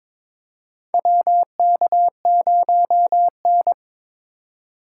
Morse code, heard as a keyed tone, about 21 wpm, 700 Hz. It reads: WK0N
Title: USA Call Signs - 2x Character Spacing - 22wpm